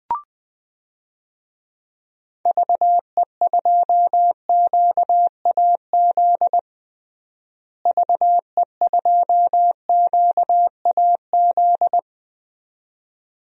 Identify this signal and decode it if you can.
Morse code, heard as a keyed tone, about 20 wpm, 700 Hz. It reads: VE2QAZ VE2QAZ
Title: DXCC-100 Call Signs - 6 Characters - Repeated 2x - 20wpm